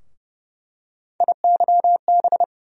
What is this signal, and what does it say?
Morse code, heard as a keyed tone, about 30 wpm, 700 Hz. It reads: IYB
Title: USA Call Sign Postfixes - 3 Letters 30wpm